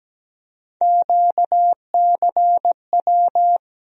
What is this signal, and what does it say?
Morse code, heard as a keyed tone, about 17 wpm, 700 Hz. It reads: QCW
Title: USA Call Sign Postfixes - 3 Letters 17wpm